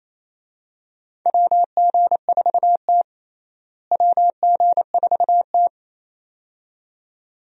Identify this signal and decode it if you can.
Morse code, heard as a keyed tone, about 28 wpm, 700 Hz. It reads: WG4T WG4T
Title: USA Call Signs - 4 Characters - Repeated 2x - 28wpm